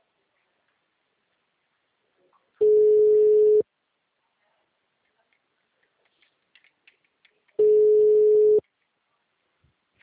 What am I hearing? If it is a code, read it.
Ringback tone (Europe, 425 Hz).